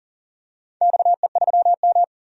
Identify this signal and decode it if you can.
Morse code, heard as a keyed tone, about 40 wpm, 700 Hz. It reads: XE3M
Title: DXCC-100 Call Signs - 4 Characters - 40wpm